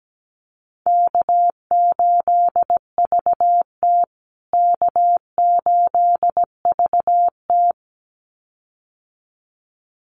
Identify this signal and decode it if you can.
Morse code, heard as a keyed tone, about 17 wpm, 700 Hz. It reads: K8VT K8VT
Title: USA Call Signs Repeated 2x 17wpm